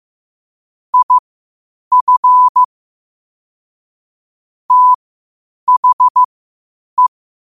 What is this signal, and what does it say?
Morse code, heard as a keyed tone, about 15 wpm, spaced as for 8 wpm, 1000 Hz. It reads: IF THE
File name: qotdfiles / 8wpm